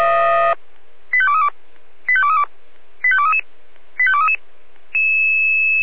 Five-tone selective call (ZVEI), heard as quick runs of five tones, tones of about 70 ms. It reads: 85232 85232 85233 85233
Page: /496/SDR/ZVEI/